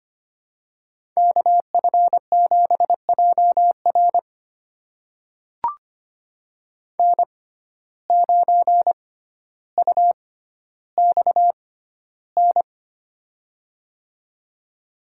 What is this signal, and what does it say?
Morse code, heard as a keyed tone, about 25 wpm, spaced as for 13 wpm, 700 Hz. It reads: KF7JR N9UXN
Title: USA Call Signs - 5 Characters - 6x Character Spacing - 25wpm